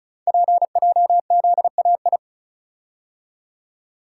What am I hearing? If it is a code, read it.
Morse code, heard as a keyed tone, about 36 wpm, 700 Hz. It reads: PJZAI
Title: Code Groups - 5 Letters - 35wpm